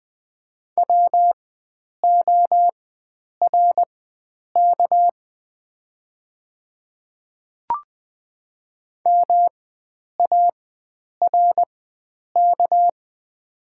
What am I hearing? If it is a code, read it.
Morse code, heard as a keyed tone, about 20 wpm, spaced as for 9 wpm, 700 Hz. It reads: WORK MARK
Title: 4x Character Spacing - Top 300 Words - 20wpm